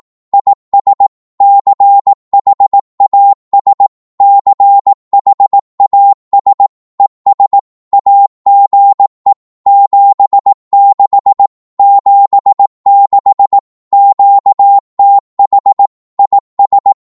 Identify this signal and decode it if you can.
Morse code, heard as a keyed tone, about 18 wpm, 800 Hz. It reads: IS CHAS CHAS ES AGE 76 76 QTH IS
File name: Morse / qso-18wpm